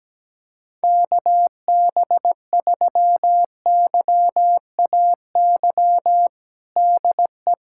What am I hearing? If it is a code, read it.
Morse code, heard as a keyed tone, about 17 wpm, 700 Hz. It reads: KB3YAY DE
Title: USA Callsign DE Callsign - 6 Characters - 17wpm